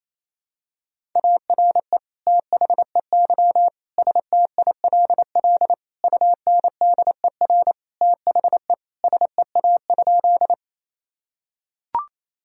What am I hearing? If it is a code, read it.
Morse code, heard as a keyed tone, about 28 wpm, 700 Hz. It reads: ARE THEY STILL UNDER THE SEA?
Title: Sentences from Top 1000 Words - Review Part 1 - 28wpm